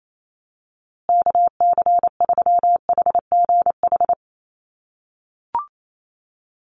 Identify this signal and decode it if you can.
Morse code, heard as a keyed tone, about 28 wpm, 700 Hz. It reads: KC3HGH
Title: USA Call Signs Repeated 2x 28wpm